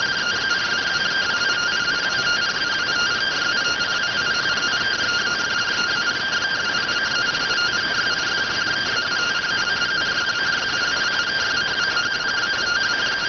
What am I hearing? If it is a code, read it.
Two-tone teleprinter signal, about 170 Hz shift.